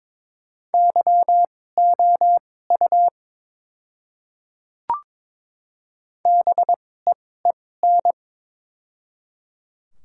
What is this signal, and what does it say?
Morse code, heard as a keyed tone, about 22 wpm, spaced as for 16 wpm, 700 Hz. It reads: YOU BEEN
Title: 2x Character Spacing - Top 100 Words - 22wpm